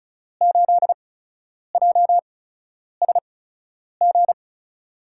Morse code, heard as a keyed tone, about 36 wpm, spaced as for 10 wpm, 700 Hz. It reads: 8JSG
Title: USA Call Signs - 5 Characters - 8x Character Spacing - 35wpm